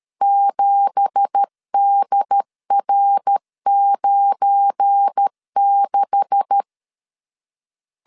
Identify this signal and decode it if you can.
Morse code, heard as a keyed tone, about 13 wpm, 800 Hz. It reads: 7DR96
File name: LEZIONI / 9 2867 12wpm 2322